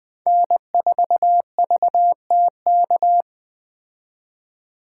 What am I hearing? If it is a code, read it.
Morse code, heard as a keyed tone, about 20 wpm, 700 Hz. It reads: N4VTK
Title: Speed Racing - USA 1x3 Call Signs 17wpm